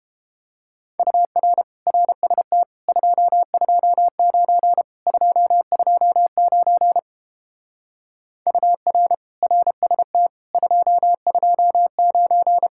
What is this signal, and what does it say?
Morse code, heard as a keyed tone, about 30 wpm, 700 Hz. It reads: UR RST 229 229 UR RST 229